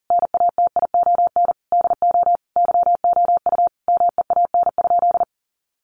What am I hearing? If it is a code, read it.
Morse code, heard as a keyed tone, about 40 wpm, 700 Hz. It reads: NATION DO YOU MEAN?
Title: Sentences from Top 900 Words 40wpm